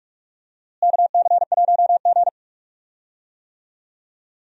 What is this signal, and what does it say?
Morse code, heard as a keyed tone, about 46 wpm, 700 Hz. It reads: KC1G
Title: USA Call Signs - 4 Characters - Repeated 3x - 45wpm